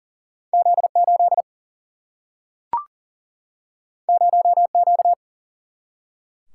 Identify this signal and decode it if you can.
Morse code, heard as a keyed tone, about 40 wpm, 700 Hz. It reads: Z8 0Q
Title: Single-Letter Single-Number 40wpm